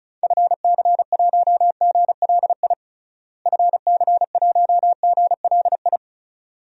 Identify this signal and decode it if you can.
Morse code, heard as a keyed tone, about 36 wpm, 700 Hz. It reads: FC1GLI FC1GLI
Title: DXCC-100 Call Signs - 6 Characters - Repeated 2x - 35wpm